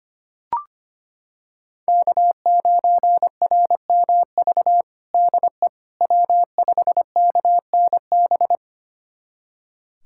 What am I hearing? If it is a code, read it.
Morse code, heard as a keyed tone, about 25 wpm, 700 Hz. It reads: K9RMV DE W5KNB
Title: USA Callsign DE Callsign - 5 Characters - 25wpm